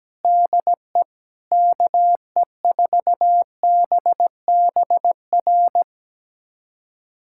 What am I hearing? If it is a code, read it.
Morse code, heard as a keyed tone, about 17 wpm, 700 Hz. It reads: DE KE4BBR
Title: USA Callsign DE Callsign - 6 Characters - 17wpm